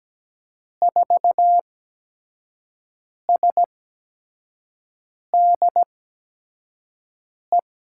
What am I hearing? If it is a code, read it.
Morse code, heard as a keyed tone, about 17 wpm, spaced as for 5 wpm, 700 Hz. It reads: 4SDE
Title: USA Call Signs - 8x Character Spacing - 17wpm